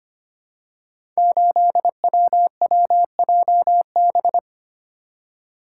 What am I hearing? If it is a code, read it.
Morse code, heard as a keyed tone, about 25 wpm, 700 Hz. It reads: 8WWJB
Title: Code Groups - 5 Letter:Numbers - 25wpm